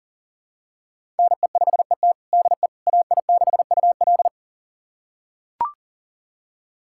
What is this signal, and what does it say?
Morse code, heard as a keyed tone, about 37 wpm, 700 Hz. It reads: NE5ET DE AI6UL
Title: USA Callsign DE Callsign - 5 Characters - 40wpm